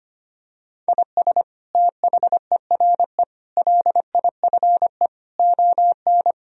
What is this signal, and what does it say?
Morse code, heard as a keyed tone, about 25 wpm, 700 Hz. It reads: IS THERE LIFE ON
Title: Sentences from Top 1000 Words - Review Part 3 - 25wpm